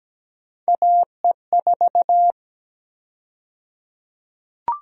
Morse code, heard as a keyed tone, about 17 wpm, 700 Hz. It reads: AE4
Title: USA Call Sign Prefixes - 17wpm